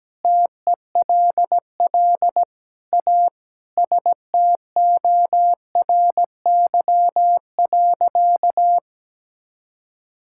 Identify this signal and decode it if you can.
Morse code, heard as a keyed tone, about 17 wpm, 700 Hz. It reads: TELL A STORY.
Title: Sentences from Top 1000 Words - Review Part 6 - 17wpm